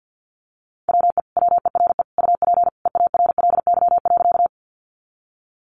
Morse code, heard as a keyed tone, about 50 wpm, 700 Hz. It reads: WE WERE UP EARLY.
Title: Sentences from Top 400 Words 50wpm